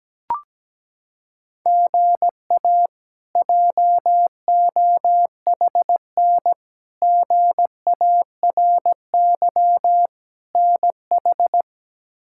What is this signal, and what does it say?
Morse code, heard as a keyed tone, about 17 wpm, 700 Hz. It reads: GA JOHN GARY NH
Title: K1USN SST QSO Practice - 17wpm